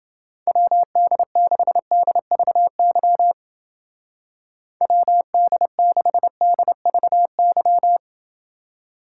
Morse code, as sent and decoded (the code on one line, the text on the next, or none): .-- -.. -.... -.. ...- -.-- / .-- -.. -.... -.. ...- -.--
WD6DVY WD6DVY